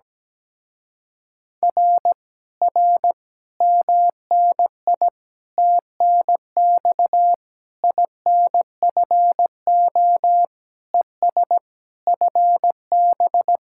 Morse code, heard as a keyed tone, about 17 wpm, 700 Hz. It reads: R R MNI TNX INFO ES FB
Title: Speed-Racing Common QSO Phrases Encore - 17wpm